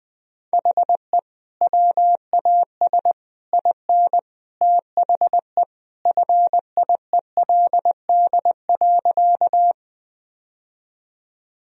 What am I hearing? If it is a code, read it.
Morse code, heard as a keyed tone, about 20 wpm, 700 Hz. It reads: HE WAS IN THE FIELD.
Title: Sentences from Top 1000 Words - Review Part 2 - 20wpm